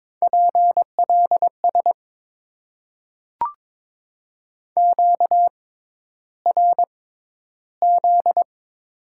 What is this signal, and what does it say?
Morse code, heard as a keyed tone, about 22 wpm, spaced as for 11 wpm, 700 Hz. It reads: PLS QRZ
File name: Top 90 QSO Elements - 6x Character Spacing - 22wpm